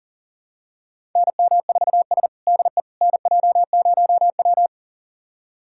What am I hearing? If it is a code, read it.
Morse code, heard as a keyed tone, about 37 wpm, 700 Hz. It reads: NM4S DE NJ0W
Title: USA Callsign DE Callsign - 4 Characters - 40wpm